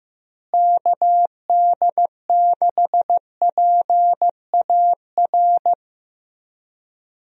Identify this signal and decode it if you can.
Morse code, heard as a keyed tone, about 15 wpm, 700 Hz. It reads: KD6PAR
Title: USA Call Signs - 6 Characters - Repeated 3x - 15wpm